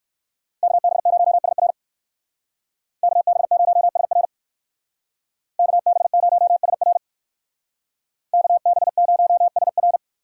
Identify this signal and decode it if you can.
Morse code, heard as a keyed tone, about 67 wpm, 700 Hz. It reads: KB0SR KB0SR KB0SR KB0SR